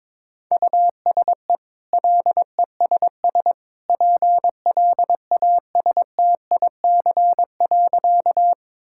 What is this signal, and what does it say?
Morse code, heard as a keyed tone, about 22 wpm, 690 Hz. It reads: USE LESS PLASTIC.
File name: practice / 22wpm / Easy Sentences with Hints - 22wpm